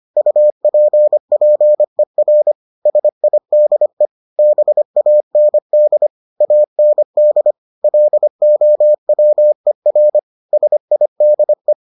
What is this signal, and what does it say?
Morse code, heard as a keyed tone, about 25 wpm, 600 Hz. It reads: UPPER SIDE BAND AND LOWER SIDE